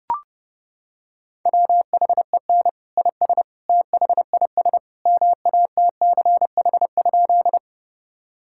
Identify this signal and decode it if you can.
Morse code, heard as a keyed tone, about 30 wpm, 700 Hz. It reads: WHEN IS THIS MATCH?
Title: Sentences from Top 1600 Words 30wpm